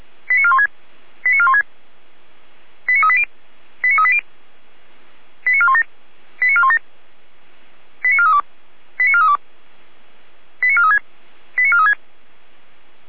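Five-tone selective call (ZVEI), heard as quick runs of five tones, tones of about 70 ms. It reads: 89416 89416 89380 89380 89417 89417 89432 89432 89436 89436
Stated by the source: ZVEI five-tone call 89356 received on an SDR.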